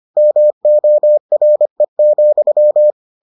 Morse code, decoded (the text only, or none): MORE,